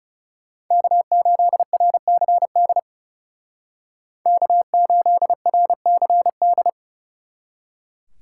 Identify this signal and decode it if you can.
Morse code, heard as a keyed tone, about 35 wpm, 700 Hz. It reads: K8RCD K8RCD
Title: Speed Racing - USA 1x3 Call Signs 30wpm